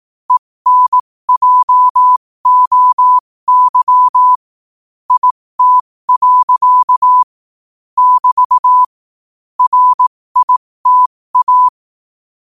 Morse code, heard as a keyed tone, about 18 wpm, spaced as for 15 wpm, 1000 Hz. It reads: ENJOY IT. <BT> RITA
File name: qotdfiles / 15wpm